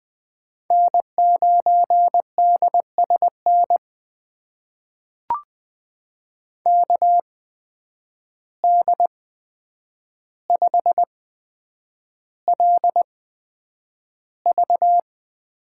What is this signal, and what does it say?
Morse code, heard as a keyed tone, about 20 wpm, spaced as for 9 wpm, 700 Hz. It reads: N9DSN KD5LV
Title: USA Call Signs - 5 Characters - 8x Character Spacing - 20wpm